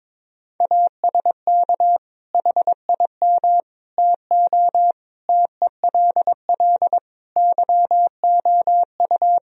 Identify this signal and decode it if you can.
Morse code, heard as a keyed tone, about 22 wpm, 700 Hz. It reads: ASK HIM TO TELL YOU
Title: Sentences from Top 1000 Words - Review Part 3 - 22wpm